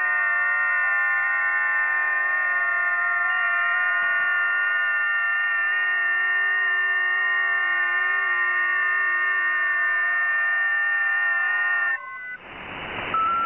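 FT8.
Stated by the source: FT8 signal received via WebSDR, a test case.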